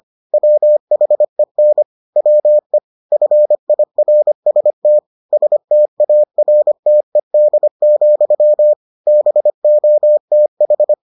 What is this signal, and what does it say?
Morse code, heard as a keyed tone, about 25 wpm, 600 Hz. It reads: WHEN WE FIRST STARTED, BOTH